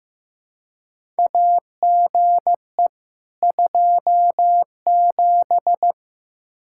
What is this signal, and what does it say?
Morse code, heard as a keyed tone, about 15 wpm, 700 Hz. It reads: AGE 27
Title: K1USN SST QSO Practice 15wpm